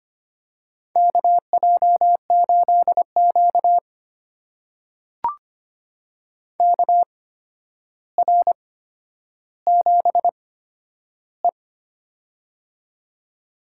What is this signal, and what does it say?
Morse code, heard as a keyed tone, about 25 wpm, spaced as for 11 wpm, 700 Hz. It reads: KJ8Q KR7E
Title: USA Call Signs - 4 Characters - 8x Character Spacing - 25wpm